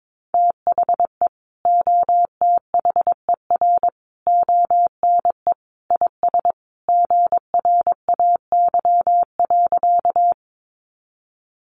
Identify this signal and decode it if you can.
Morse code, heard as a keyed tone, about 22 wpm, 700 Hz. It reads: THE OTHER ONE IS GRAY.
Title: Additional Easy Phrases 22wpm